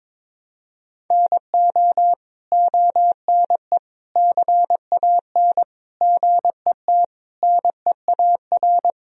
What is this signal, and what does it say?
Morse code, heard as a keyed tone, about 21 wpm, 700 Hz. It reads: NO ONE CAN GET NEAR
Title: Sentences from Top 1000 Words - Review Part 1 - 22wpm